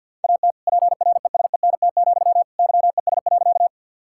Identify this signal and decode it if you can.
Morse code, heard as a keyed tone, about 50 wpm, 700 Hz. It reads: AT PRESENT, YES.
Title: Sentences from Top 1000 Words - Review Part 5 - 50wpm